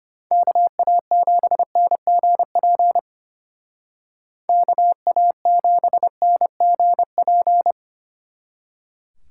Morse code, heard as a keyed tone, about 31 wpm, 700 Hz. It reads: KA7NGP KA7NGP
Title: Speed Racing - USA 2x3 Call Signs 25wpm